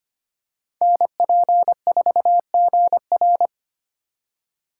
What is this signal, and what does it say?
Morse code, heard as a keyed tone, about 25 wpm, 700 Hz. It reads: NP4GR
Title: USA Call Signs - 5 Characters - 2x Character Spacing - 25wpm